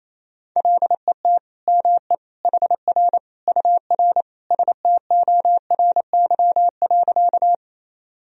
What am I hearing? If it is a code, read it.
Morse code, heard as a keyed tone, about 28 wpm, 700 Hz. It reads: LET ME HR UR STORY.
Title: Sentences from Top 300 Words and Abbreviations 28wpm